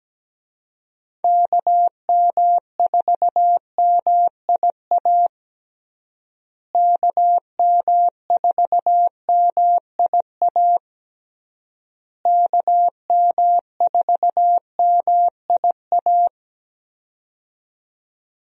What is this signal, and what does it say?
Morse code, heard as a keyed tone, about 17 wpm, 700 Hz. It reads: KM4MIA KM4MIA KM4MIA
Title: USA Call Signs - 6 Characters - Repeated 3x - 17wpm